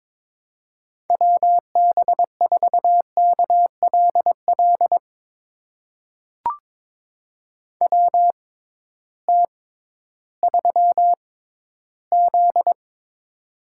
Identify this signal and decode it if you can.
Morse code, heard as a keyed tone, about 22 wpm, spaced as for 13 wpm, 700 Hz. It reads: WB4KLL WT3Z
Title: USA Call Signs - 6x Character Spacing - 22wpm